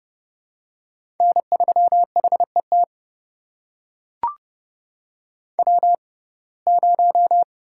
Morse code, heard as a keyed tone, about 31 wpm, spaced as for 22 wpm, 700 Hz. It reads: N3HET W0
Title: USA Call Signs - 5 Characters - 6x Character Spacing - 30wpm